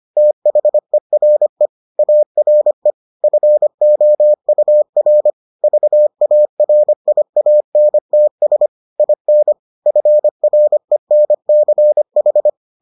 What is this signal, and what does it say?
Morse code, heard as a keyed tone, about 25 wpm, 600 Hz. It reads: THERE ARE FOUR VARIANTS IN FRENCH